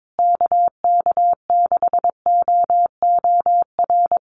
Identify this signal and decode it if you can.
Morse code, heard as a keyed tone, about 22 wpm, 700 Hz. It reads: KK6OOR
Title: USA Call Signs Repeated 3x 22wpm